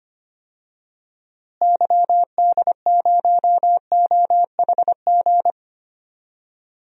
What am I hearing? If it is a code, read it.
Morse code, heard as a keyed tone, about 25 wpm, 700 Hz. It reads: YD0OHG
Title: DXCC-100 Call Signs - 6 Characters - Repeated 2x - 25wpm